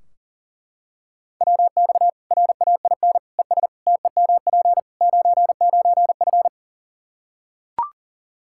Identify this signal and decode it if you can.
Morse code, heard as a keyed tone, about 40 wpm, 700 Hz. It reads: WX RAIN ES TEMP 99F
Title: Practice QSO - Exchange 2 - Info - 40wpm